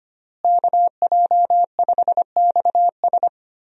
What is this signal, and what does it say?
Morse code, heard as a keyed tone, about 25 wpm, 700 Hz. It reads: KJ5XS
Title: USA Call Signs - 5 Characters - Repeated 2x - 25wpm